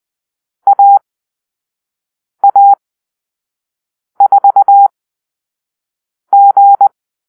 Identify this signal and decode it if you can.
Morse code, heard as a keyed tone, about 20 wpm, spaced as for 5 wpm, 800 Hz. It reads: AA4G
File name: arrl / Call-Sign Drill-4